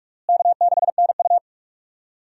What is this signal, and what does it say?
Morse code, heard as a keyed tone, about 46 wpm, 700 Hz. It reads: K6NU
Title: USA Call Signs - 4 Characters - Repeated 2x - 45wpm